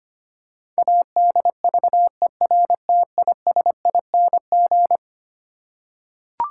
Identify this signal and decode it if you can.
Morse code, heard as a keyed tone, about 25 wpm, 700 Hz. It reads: ADVERTISING
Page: morse-code-ninja-practice-25wpm